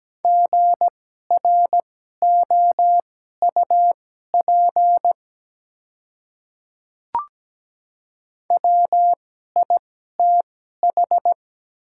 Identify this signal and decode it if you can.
Morse code, heard as a keyed tone, about 17 wpm, spaced as for 12 wpm, 700 Hz. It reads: GROUP WITH
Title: 2x Character Spacing - Top 300 Words - 17wpm